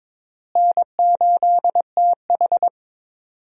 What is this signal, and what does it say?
Morse code, heard as a keyed tone, about 22 wpm, 700 Hz. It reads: N8TH
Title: USA Call Signs - 4 Characters - Repeated 3x - 22wpm